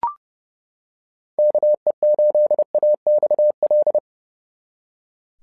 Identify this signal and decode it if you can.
Morse code, heard as a keyed tone, about 30 wpm, 590 Hz. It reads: KE8AXL
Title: SOTA QSO Practice 30wpm